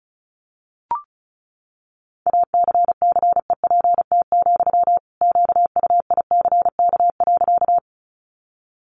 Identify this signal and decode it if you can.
Morse code, heard as a keyed tone, about 34 wpm, 700 Hz. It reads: ACCEPT, QUICK.
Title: Sentences from Top 1300 Words 35wpm